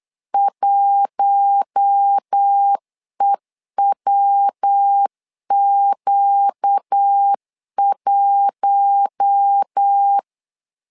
Morse code, heard as a keyed tone, about 8 wpm, 800 Hz. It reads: 1EWQ1
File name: LEZIONI / 4 Q1 8wpm 3035